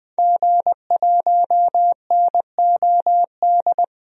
Morse code, heard as a keyed tone, about 20 wpm, 700 Hz. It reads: G1NOD
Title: DXCC-100 Call Signs - 5 Characters - Repeated 2x - 20wpm